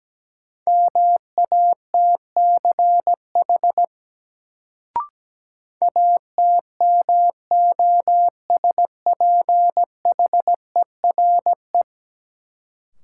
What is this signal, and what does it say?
Morse code, heard as a keyed tone, about 17 wpm, 700 Hz. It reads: MATCH ATMOSPHERE
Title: Sentences from Top 1600 Words - New Word Drill - 17wpm